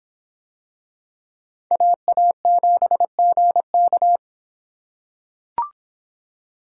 Morse code, heard as a keyed tone, about 26 wpm, 700 Hz. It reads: AA7GK